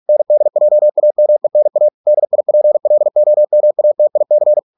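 Morse code, heard as a keyed tone, about 46 wpm, 600 Hz. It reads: NDJAMENA DIPLOMATIC
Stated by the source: Morse practice at 45 wpm, the novel Never by Ken Follett in English, from CW med Gnister.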